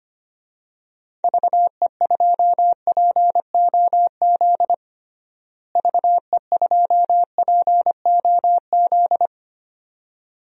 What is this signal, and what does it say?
Morse code, heard as a keyed tone, about 25 wpm, 700 Hz. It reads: VE2POZ VE2POZ
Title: DXCC-100 Call Signs - 6 Characters - Repeated 2x - 25wpm